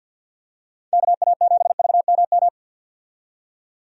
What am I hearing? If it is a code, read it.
Morse code, heard as a keyed tone, about 48 wpm, 700 Hz. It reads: KA7VMM
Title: USA Call Signs - 6 Characters - 50wpm